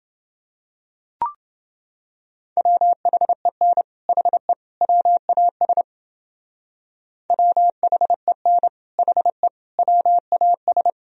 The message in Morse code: WHEN HE WAS WHEN HE WAS

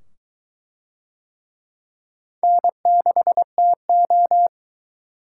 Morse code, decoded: N6TO

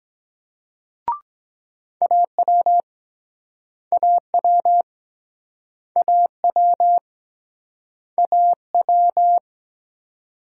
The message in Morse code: AW AW AW AW

